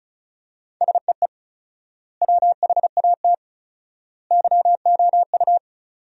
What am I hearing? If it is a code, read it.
Morse code, heard as a keyed tone, about 36 wpm, 700 Hz. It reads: SEE WHAT YOU
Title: 4x Word-Spacing - Sentences from Top 500 Words - 35wpm